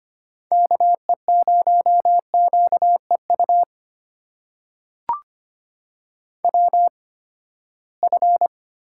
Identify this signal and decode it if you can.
Morse code, heard as a keyed tone, about 25 wpm, spaced as for 17 wpm, 700 Hz. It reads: KE0QEU WF
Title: USA Call Signs - 8x Character Spacing - 25wpm